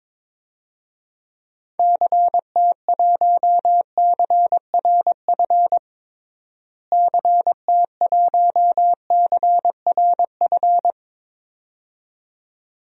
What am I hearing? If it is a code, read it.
Morse code, heard as a keyed tone, about 22 wpm, 700 Hz. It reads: CT1CRF CT1CRF
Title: DXCC-100 Call Signs - 6 Characters - Repeated 2x - 22wpm